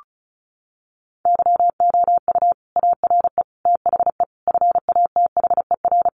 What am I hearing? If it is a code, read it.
Morse code, heard as a keyed tone, about 34 wpm, 700 Hz. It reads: YOU ARE THE FATHER
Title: Sentences from Top 1500 Words 35wpm